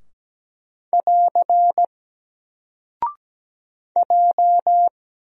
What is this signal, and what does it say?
Morse code, heard as a keyed tone, about 17 wpm, 700 Hz. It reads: <AR> J
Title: Single Character (Letter:Number:Punctuation:Prosign) 17wpm